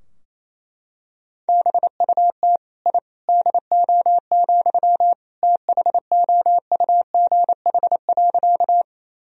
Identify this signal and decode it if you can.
Morse code, heard as a keyed tone, about 28 wpm, 700 Hz. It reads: BUT I DO, THOUGH.
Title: Sentences from Top 1000 Words - Review Part 2 - 28wpm